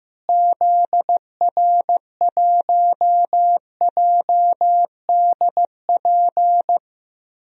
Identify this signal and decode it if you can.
Morse code, heard as a keyed tone, about 15 wpm, 700 Hz. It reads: ZR1JDP
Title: DXCC-100 Call Signs - 6 Characters - Repeated 2x - 15wpm